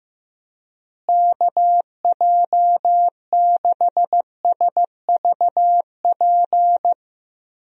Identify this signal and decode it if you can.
Morse code, heard as a keyed tone, about 16 wpm, 700 Hz. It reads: KJ6SVP
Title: USA Call Signs - 6 Characters - Repeated 2x - 15wpm